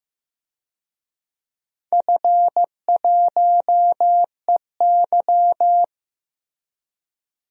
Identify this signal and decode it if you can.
Morse code, heard as a keyed tone, about 15 wpm, 700 Hz. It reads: F1EY
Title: DXCC-100 Call Signs - 4 Characters - Repeated 3x - 15wpm